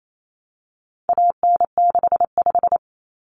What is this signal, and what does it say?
Morse code, heard as a keyed tone, about 28 wpm, 700 Hz. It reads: AN65
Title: Common Cut Numbers - 4-Digit 28wpm